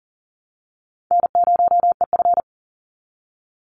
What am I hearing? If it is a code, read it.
Morse code, heard as a keyed tone, about 39 wpm, 700 Hz. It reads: N0EF
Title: Easy Call Signs - 40wpm